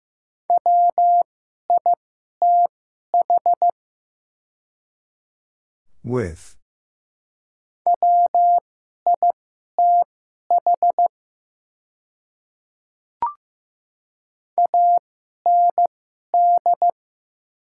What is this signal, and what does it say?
Morse code, heard as a keyed tone, about 15 wpm, spaced as for 11 wpm, 690 Hz. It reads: WITH WITH AND
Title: 2x Character Spacing - Top 100 Words - 15wpm